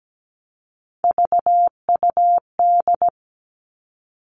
Morse code, heard as a keyed tone, about 17 wpm, 700 Hz. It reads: VUD
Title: Three Challenging Letter-Number Combinations 17wpm